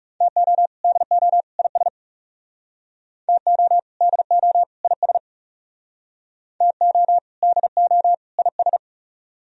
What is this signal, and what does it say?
Morse code, heard as a keyed tone, about 38 wpm, 700 Hz. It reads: TO DO IS TO DO IS TO DO IS